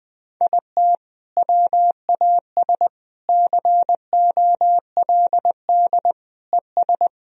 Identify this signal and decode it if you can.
Morse code, heard as a keyed tone, about 20 wpm, 700 Hz. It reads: IT WAS COLD ES
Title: Sentences from Top 500 Words and Abbreviations 20wpm